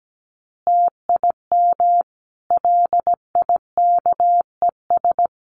Morse code, heard as a keyed tone, about 17 wpm, 700 Hz. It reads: TIM LIKES
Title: Easy Phrases 17wpm